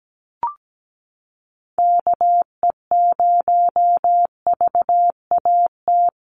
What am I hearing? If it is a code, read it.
Morse code, heard as a keyed tone, about 17 wpm, 700 Hz. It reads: KE0VAT
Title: USA Call Signs Repeated 2x 17wpm